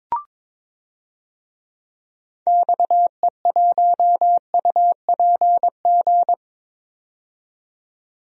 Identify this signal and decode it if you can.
Morse code, heard as a keyed tone, about 22 wpm, 700 Hz. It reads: XE1UPG
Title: DXCC-100 Call Signs - 6 Characters - 22wpm